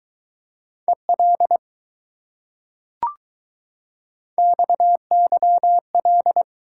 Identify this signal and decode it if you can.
Morse code, heard as a keyed tone, about 20 wpm, 700 Hz. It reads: EL XYL